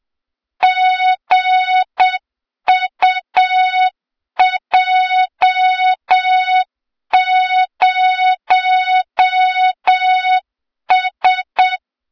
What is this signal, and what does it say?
Morse code, heard as a keyed tone, about 8 wpm, 3700 Hz, with noise. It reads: GUJ0S
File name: Morze-HA2MN / hang / 23 GYAKORLAT